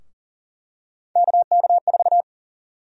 Morse code, heard as a keyed tone, about 40 wpm, 700 Hz. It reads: KK4
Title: USA Call Sign Prefixes - Repeated 2x - 40wpm